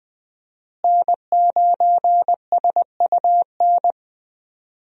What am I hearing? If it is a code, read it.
Morse code, heard as a keyed tone, about 20 wpm, 700 Hz. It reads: N9SUN
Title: USA Call Signs - 5 Characters - Repeated 3x - 20wpm